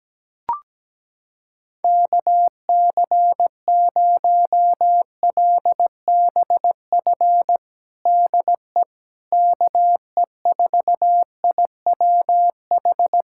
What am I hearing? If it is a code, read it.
Morse code, heard as a keyed tone, about 17 wpm, 700 Hz. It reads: KC0LBF DE KE4IWH
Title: USA Callsign DE Callsign - 6 Characters - 17wpm